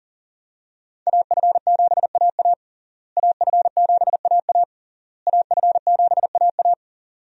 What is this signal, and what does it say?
Morse code, heard as a keyed tone, about 37 wpm, 700 Hz. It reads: AF7AA AF7AA AF7AA
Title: USA Call Signs - 5 Characters - Repeated 3x - 40wpm